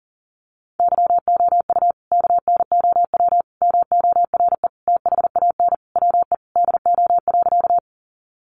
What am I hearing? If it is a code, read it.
Morse code, heard as a keyed tone, about 40 wpm, 700 Hz. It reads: YOU KNOW MORE THAN WE DO.
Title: Sentences from Top 100 Words 40wpm